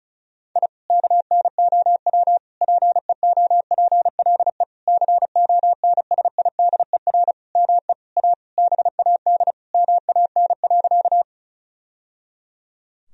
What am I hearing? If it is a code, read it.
Morse code, heard as a keyed tone, about 34 wpm, 700 Hz. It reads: I KNOW PEOPLE CONSIDER ME A BAD MAN.
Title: Sentences from Top 1000 Words - Review Part 4 - 35wpm